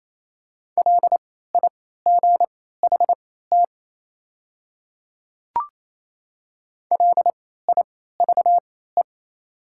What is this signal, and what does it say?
Morse code, heard as a keyed tone, about 28 wpm, spaced as for 16 wpm, 700 Hz. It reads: LIGHT LIVE
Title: 3x Character Spacing - Top 200 Words - 28wpm